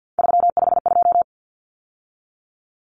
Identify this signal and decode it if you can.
Morse code, heard as a keyed tone, about 50 wpm, 700 Hz. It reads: 352
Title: Numbers - 3-Digit 50wpm